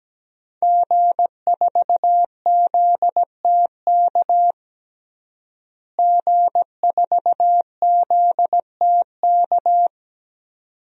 Morse code, heard as a keyed tone, about 17 wpm, 700 Hz. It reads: G4ZTK G4ZTK
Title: DXCC-100 Call Signs - 5 Characters - Repeated 2x - 17wpm